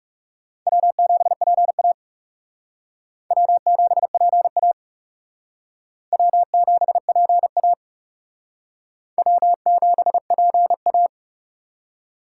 Morse code, heard as a keyed tone, about 40 wpm, 700 Hz. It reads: W7PA W7PA W7PA W7PA